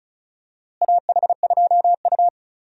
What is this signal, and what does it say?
Morse code, heard as a keyed tone, about 36 wpm, 700 Hz. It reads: AH2U
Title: USA Call Signs - 4 Characters - Repeated 3x - 35wpm